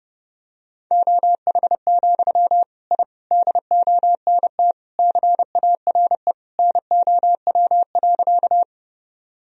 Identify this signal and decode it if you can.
Morse code, heard as a keyed tone, about 30 wpm, 700 Hz. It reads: OH, I DONT CARE NOW.
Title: Sentences from Top 1000 Words - Review Part 1 - 30wpm